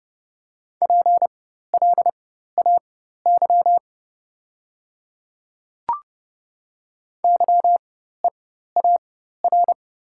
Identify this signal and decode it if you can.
Morse code, heard as a keyed tone, about 29 wpm, spaced as for 14 wpm, 690 Hz. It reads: PLAY YEAR
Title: 4x Character Spacing - Top 200 Words - 30wpm